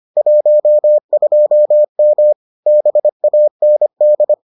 Morse code, heard as a keyed tone, about 25 wpm, 600 Hz. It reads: 12M BAND